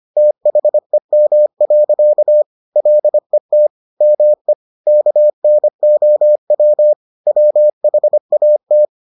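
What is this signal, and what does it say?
Morse code, heard as a keyed tone, about 25 wpm, 600 Hz. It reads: THEM. LET ME KNOW WHAT
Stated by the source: Morse edition of the Foundations of Amateur Radio episodes.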